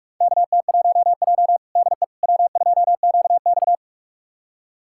Morse code, heard as a keyed tone, about 47 wpm, 700 Hz. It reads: KT1J DE W2QX
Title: USA Callsign DE Callsign - 4 Characters - 45wpm